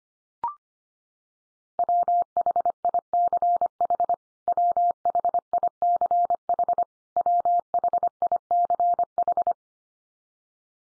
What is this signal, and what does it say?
Morse code, heard as a keyed tone, about 25 wpm, 700 Hz. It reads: WHICH WHICH WHICH
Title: Top 100 Words 25wpm x3